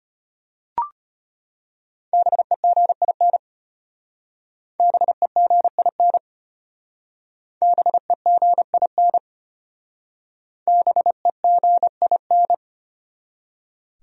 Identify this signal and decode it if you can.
Morse code, heard as a keyed tone, about 33 wpm, 700 Hz. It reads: BEGIN BEGIN BEGIN BEGIN